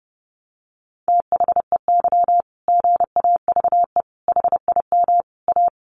Morse code, heard as a keyed tone, about 30 wpm, 700 Hz. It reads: THEY GAVE HIM A
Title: Sentences from Top 1500 Words 30wpm